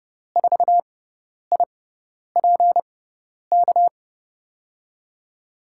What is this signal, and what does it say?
Morse code, heard as a keyed tone, about 30 wpm, spaced as for 10 wpm, 700 Hz. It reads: 4IPK
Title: USA Call Signs - 6x Character Spacing - 30wpm